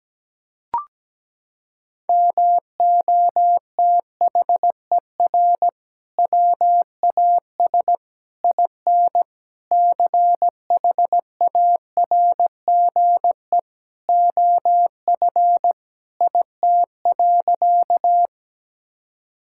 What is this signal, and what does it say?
Morse code, heard as a keyed tone, about 17 wpm, 700 Hz. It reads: MOTHER WAS IN CHARGE OF IT.
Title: Sentences from Top 1000 Words - Review Part 6 - 17wpm